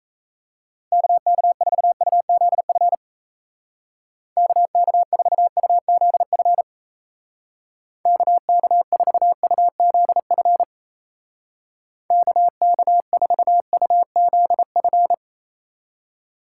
Morse code, heard as a keyed tone, about 36 wpm, 700 Hz. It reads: KK4UZF KK4UZF KK4UZF KK4UZF